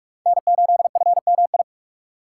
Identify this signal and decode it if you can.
Morse code, heard as a keyed tone, about 46 wpm, 700 Hz. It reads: N8FMI
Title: USA Call Signs - 5 Characters - 2x Character Spacing - 45wpm